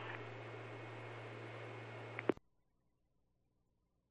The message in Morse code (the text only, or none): none